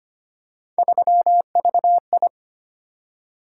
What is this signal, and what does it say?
Morse code, heard as a keyed tone, about 25 wpm, 700 Hz. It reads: 3VI
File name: Three Letter-Number Combinations - 25wpm